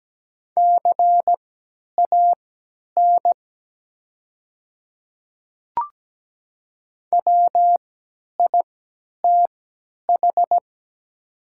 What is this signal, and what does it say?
Morse code, heard as a keyed tone, about 17 wpm, spaced as for 10 wpm, 700 Hz. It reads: CAN WITH
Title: 3x Character Spacing - Top 100 Words - 17wpm